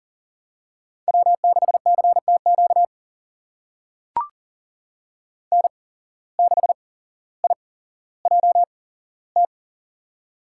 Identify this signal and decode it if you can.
Morse code, heard as a keyed tone, about 40 wpm, spaced as for 17 wpm, 700 Hz. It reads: W6CTQ N6IJT
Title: USA Call Signs - 5 Characters - 8x Character Spacing - 40wpm